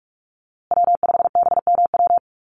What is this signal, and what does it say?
Morse code, heard as a keyed tone, about 44 wpm, 700 Hz. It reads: W5BMW
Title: USA Call Signs Repeated 2x 45wpm